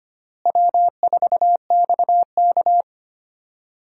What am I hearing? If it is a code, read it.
Morse code, heard as a keyed tone, about 25 wpm, 700 Hz. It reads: W4XK
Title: USA Call Signs - 4 Characters - Repeated 3x - 25wpm